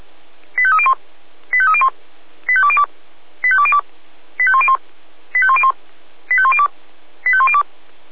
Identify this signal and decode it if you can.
Five-tone selective call (ZVEI), heard as quick runs of five tones, tones of about 70 ms. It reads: 85301 85301 85202 85202 85101 85101 85102 85102